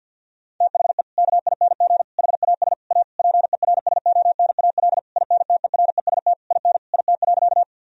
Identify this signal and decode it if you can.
Morse code, heard as a keyed tone, about 50 wpm, 700 Hz. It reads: THE KING HAS A PERSONAL INTEREST IN IT.
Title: Sentences from Top 1000 Words - Review Part 4 - 50wpm